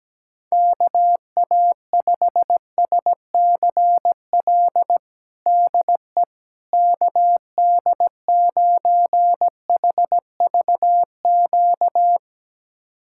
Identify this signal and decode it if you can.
Morse code, heard as a keyed tone, about 17 wpm, 700 Hz. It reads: KA5SCL DE KD9HVQ
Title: USA Callsign DE Callsign - 6 Characters - 17wpm